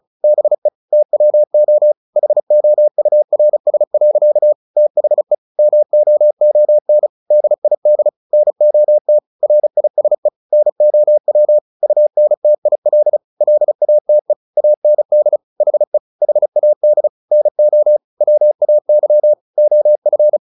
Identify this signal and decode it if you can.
Morse code, heard as a keyed tone, about 34 wpm, 600 Hz. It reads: BE TWO HOURS. THE MOON DID NOT RISE NOW UNTIL LATE AND HE HAD NO WAY OF
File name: the old man and the sea 2 35wpm